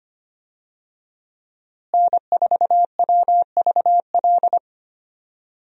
Morse code, heard as a keyed tone, about 25 wpm, 700 Hz. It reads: N4WVL
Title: USA Call Signs - 5 Characters - Repeated 2x - 25wpm